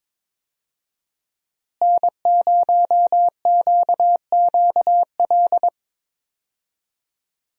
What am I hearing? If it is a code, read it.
Morse code, heard as a keyed tone, about 22 wpm, 700 Hz. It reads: N0QQL
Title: USA Call Signs - 5 Characters - Repeated 3x - 22wpm